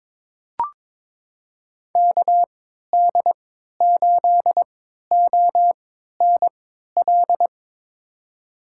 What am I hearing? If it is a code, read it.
Morse code, heard as a keyed tone, about 22 wpm, spaced as for 13 wpm, 700 Hz. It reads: KD8ONL
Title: USA Call Signs - 6 Characters - 3x Character Spacing - 22wpm